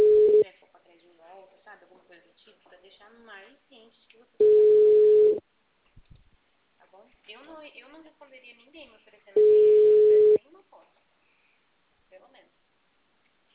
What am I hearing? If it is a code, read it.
Ringback tone (Europe, 425 Hz), with speech.